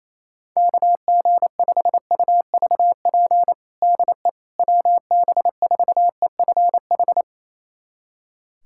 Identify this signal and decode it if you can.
Morse code, heard as a keyed tone, about 27 wpm, 700 Hz. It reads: KG5UVP DE WB4EFH
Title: USA Callsign DE Callsign 28wpm